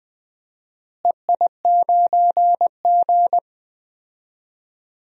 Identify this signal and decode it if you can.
Morse code, heard as a keyed tone, about 20 wpm, 700 Hz. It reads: EI9G
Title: DXCC-100 Call Signs - 4 Characters - Repeated 2x - 20wpm